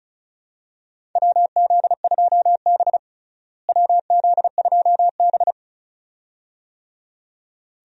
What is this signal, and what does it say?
Morse code, heard as a keyed tone, about 36 wpm, 700 Hz. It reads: WZ2B WZ2B
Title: USA Call Signs - 4 Characters - Repeated 2x - 35wpm